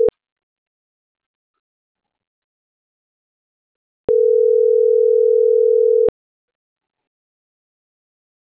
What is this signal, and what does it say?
Ringback tone (North America), 440 + 480 Hz.